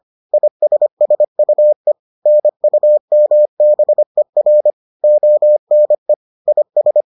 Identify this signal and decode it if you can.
Morse code, heard as a keyed tone, about 25 wpm, 600 Hz. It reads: ISSUE NUMBER ONE IS